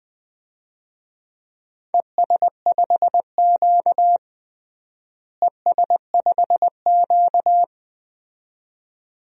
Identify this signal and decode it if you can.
Morse code, heard as a keyed tone, about 21 wpm, 700 Hz. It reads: ES5Q ES5Q
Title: DXCC-100 Call Signs - 4 Characters - Repeated 2x - 20wpm